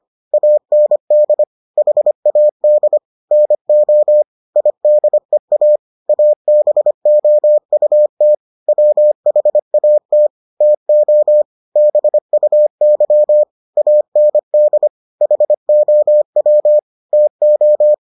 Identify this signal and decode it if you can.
Morse code, heard as a keyed tone, about 25 wpm, 600 Hz. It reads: AND HAD NO IDEA ABOUT WHAT TO BUY AND HOW TO